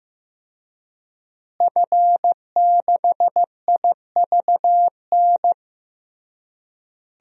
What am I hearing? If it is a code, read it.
Morse code, heard as a keyed tone, about 16 wpm, 700 Hz. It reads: F6IVN
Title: DXCC-100 Call Signs - 5 Characters - Repeated 3x - 15wpm